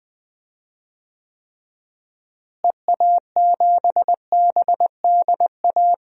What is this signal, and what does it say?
Morse code, heard as a keyed tone, about 21 wpm, 700 Hz. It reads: EA7BDA
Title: DXCC-100 Call Signs - 6 Characters - Repeated 2x - 20wpm